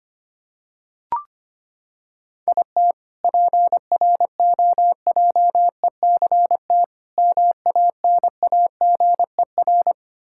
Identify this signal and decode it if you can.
Morse code, heard as a keyed tone, about 25 wpm, 700 Hz. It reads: IT PROJECT MANAGER
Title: K1USN SST QSO Practice 25wpm